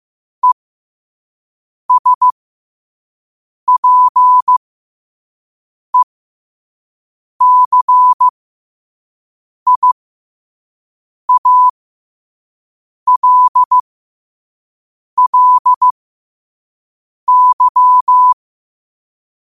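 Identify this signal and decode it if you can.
Morse code, heard as a keyed tone, about 15 wpm, spaced as for 5 wpm, 1000 Hz. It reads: ESPECIALLY
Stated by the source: Morse code sent at 5 wpm.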